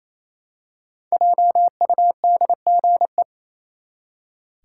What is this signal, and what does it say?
Morse code, heard as a keyed tone, about 28 wpm, 700 Hz. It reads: JUDGE
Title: Five Letter Words - 28wpm